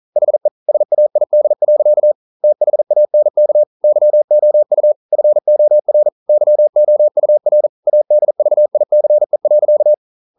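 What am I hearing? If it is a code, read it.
Morse code, heard as a keyed tone, about 44 wpm, 600 Hz. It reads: HE SAID. THANK YOU FOR YOUR ADVICE.
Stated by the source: Morse code at 40 wpm, practice text, CW med Gnister.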